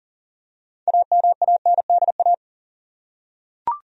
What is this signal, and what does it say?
Morse code, heard as a keyed tone, about 40 wpm, 700 Hz. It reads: AMANDA
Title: USA Common Names - 40wpm